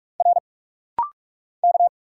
Morse code, heard as a keyed tone, about 46 wpm, 690 Hz. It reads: R K
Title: Single Letters - Rapid-Fire 45wpm